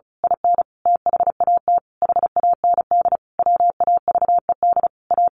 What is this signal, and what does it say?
Morse code, heard as a keyed tone, about 34 wpm, 700 Hz. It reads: IN THAT HAND WAVED A
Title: A Sound Of Thunder 35wpm